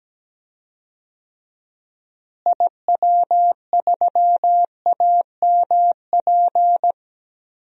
Morse code, heard as a keyed tone, about 17 wpm, 700 Hz. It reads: IW3AMP
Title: DXCC-100 Call Signs - 6 Characters - Repeated 3x - 17wpm